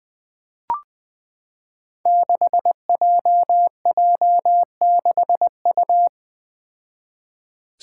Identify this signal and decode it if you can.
Morse code, heard as a keyed tone, about 20 wpm, 700 Hz. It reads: 6JJ6U
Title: Five Challenging Letter-Number Combinations 20wpm